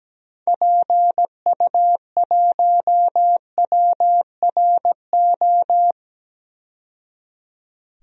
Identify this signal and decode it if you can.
Morse code, heard as a keyed tone, about 17 wpm, 700 Hz. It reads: PU1WRO
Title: DXCC-100 Call Signs - 6 Characters - 17wpm